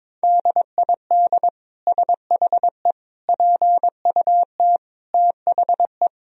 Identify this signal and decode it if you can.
Morse code, heard as a keyed tone, about 22 wpm, 700 Hz. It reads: DID SHE PUT THE
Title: Sentences from Top 1000 Words - Review Part 2 - 22wpm